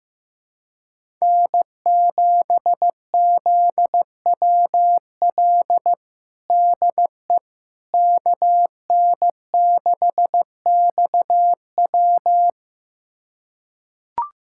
Morse code, heard as a keyed tone, about 15 wpm, 700 Hz. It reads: N7ZWL DE KN6XW
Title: USA Callsign DE Callsign - 5 Characters - 15wpm